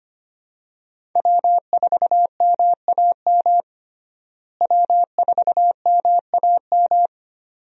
Morse code, heard as a keyed tone, about 25 wpm, 700 Hz. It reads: W4MAM W4MAM
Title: USA Call Signs - 5 Characters - Repeated 2x - 25wpm